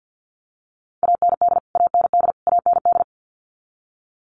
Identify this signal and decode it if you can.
Morse code, heard as a keyed tone, about 50 wpm, 700 Hz. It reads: AND AND AND
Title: Top 500 Words Repeated 50wpm